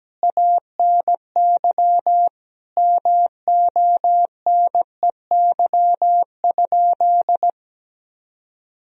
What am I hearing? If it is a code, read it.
Morse code, heard as a keyed tone, about 17 wpm, 700 Hz. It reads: ANY MONEY?
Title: Sentences from Top 400 Words and Abbreviations 17wpm